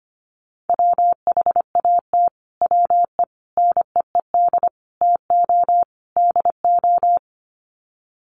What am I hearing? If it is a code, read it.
Morse code, heard as a keyed tone, about 25 wpm, 700 Hz. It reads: WHAT WE NEED TO DO
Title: Sets of 5 Words Encore 25wpm